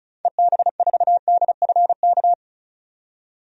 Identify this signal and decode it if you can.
Morse code, heard as a keyed tone, about 34 wpm, 700 Hz. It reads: EB4DFK
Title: DXCC-100 Call Signs - 6 Characters - 35wpm